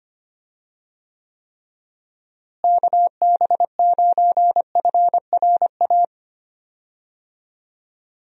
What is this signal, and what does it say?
Morse code, heard as a keyed tone, about 25 wpm, 700 Hz. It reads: KB9FRA
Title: USA Call Signs - 6 Characters - Repeated 3x - 25wpm